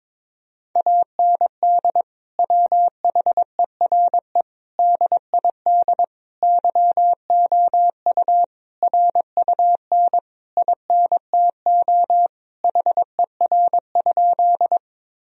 Morse code, heard as a keyed tone, about 22 wpm, 700 Hz. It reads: AND WHERE DID YOU RUN INTO HER?
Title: Sentences from Top 1000 Words - Review Part 1 - 22wpm